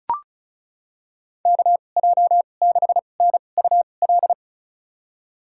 Morse code, heard as a keyed tone, about 36 wpm, spaced as for 25 wpm, 700 Hz. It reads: KJ6NUL
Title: USA Call Signs - 6 Characters - 2x Character Spacing - 35wpm